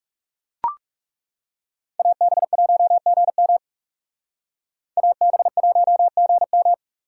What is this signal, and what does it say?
Morse code, heard as a keyed tone, about 38 wpm, 700 Hz. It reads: AB1GM AB1GM